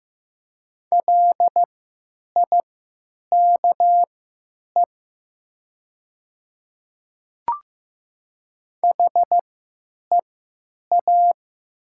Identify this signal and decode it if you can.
Morse code, heard as a keyed tone, about 15 wpm, spaced as for 9 wpm, 700 Hz. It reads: LIKE HEA
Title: 3x Character Spacing - Top 300 Words - 15wpm